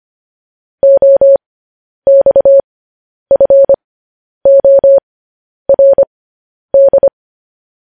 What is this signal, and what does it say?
Morse code, heard as a keyed tone, about 25 wpm, spaced as for 10 wpm, 550 Hz. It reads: OXFORD
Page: news-headlines-in-morse-code-at-10-wpm